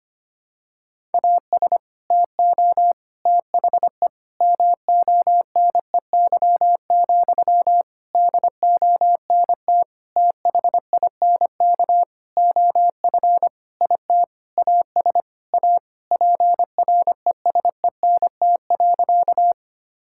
Morse code, heard as a keyed tone, about 25 wpm, 700 Hz. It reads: AS TO THE MONEY, DONT THINK OF IT AS A PRESENT.
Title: Sentences from Top 1000 Words - Review Part 5 - 25wpm